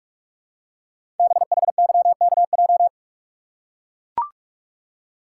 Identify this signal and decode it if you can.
Morse code, heard as a keyed tone, about 44 wpm, 700 Hz. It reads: BHYKJ
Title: Code Groups - 5 Letters - 45wpm